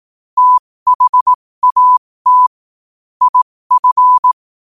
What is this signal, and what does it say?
Morse code, heard as a keyed tone, about 18 wpm, spaced as for 15 wpm, 1000 Hz. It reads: THAT IF
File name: qotdfiles / 15wpm